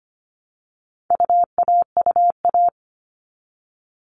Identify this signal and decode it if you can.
Morse code, heard as a keyed tone, about 25 wpm, 700 Hz. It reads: UAUA
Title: All Cut Numbers - 4-Digit 25wpm